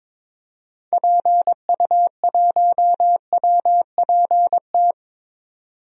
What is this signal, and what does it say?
Morse code, heard as a keyed tone, about 22 wpm, 700 Hz. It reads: PU1WPT
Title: DXCC-100 Call Signs - 6 Characters - 22wpm